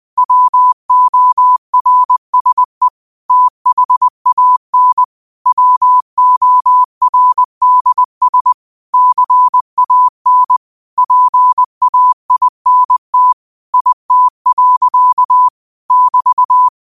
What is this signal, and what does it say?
Morse code, heard as a keyed tone, about 20 wpm, 1000 Hz. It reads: WORSE THAN WORDS CAN PAINT IT. <BT>
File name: qotdfiles / 20wpm